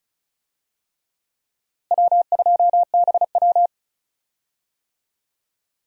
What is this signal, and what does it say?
Morse code, heard as a keyed tone, about 34 wpm, 700 Hz. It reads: W2BW
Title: USA Call Signs - 4 Characters - Repeated 3x - 35wpm